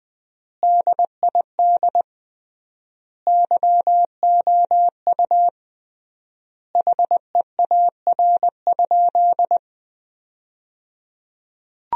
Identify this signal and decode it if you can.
Morse code, heard as a keyed tone, about 20 wpm, 700 Hz. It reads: DID YOU HEAR?
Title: 3x Word-Spacing - Sentences from Top 300 Words - 20wpm